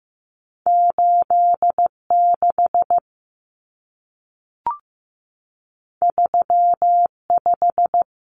Morse code, heard as a keyed tone, about 15 wpm, 700 Hz. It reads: 86 35
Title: Common Cut Numbers - 2-Digit 15wpm